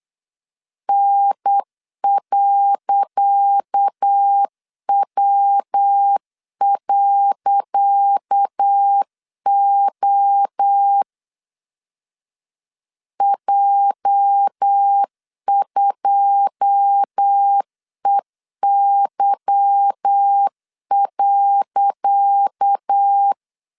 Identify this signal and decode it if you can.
Morse code, heard as a keyed tone, about 8 wpm, 800 Hz. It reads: N.W.O J2EY.
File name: LEZIONI / 10 SIGNS / 10C DOT 8wpm 3649